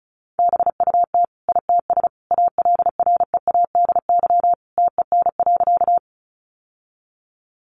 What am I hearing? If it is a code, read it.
Morse code, heard as a keyed tone, about 34 wpm, 700 Hz. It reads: BUT ITS ALREADY TEN.
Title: Sentences from Top 700 Words 35wpm